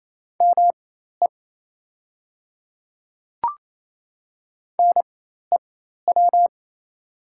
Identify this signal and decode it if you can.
Morse code, heard as a keyed tone, about 28 wpm, spaced as for 13 wpm, 690 Hz. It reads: ME NEW
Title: 4x Character Spacing - Top 200 Words - 28wpm